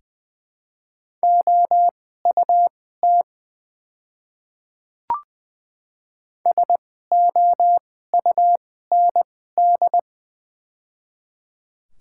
Morse code, heard as a keyed tone, about 20 wpm, spaced as for 14 wpm, 700 Hz. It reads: OUT SOUND
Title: 2x Character Spacing - Top 100 Words - 20wpm